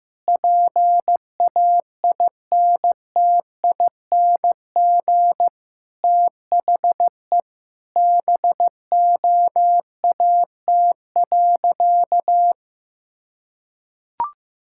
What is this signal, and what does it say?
Morse code, heard as a keyed tone, about 16 wpm, 700 Hz. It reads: PAINTING THE BOAT.
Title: Sentences from Top 1000 Words - Review Part 6 - 15wpm